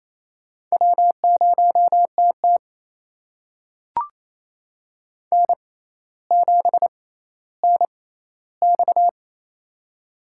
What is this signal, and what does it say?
Morse code, heard as a keyed tone, about 28 wpm, spaced as for 14 wpm, 700 Hz. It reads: W0TT N7NX
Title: USA Call Signs - 4 Characters - 6x Character Spacing - 28wpm